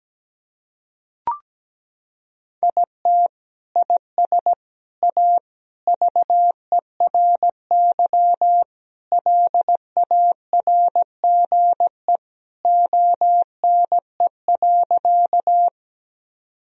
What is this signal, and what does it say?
Morse code, heard as a keyed tone, about 17 wpm, 700 Hz. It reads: IT IS A VERY LARGE ONE.
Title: Sentences from Top 1000 Words - Review Part 1 - 17wpm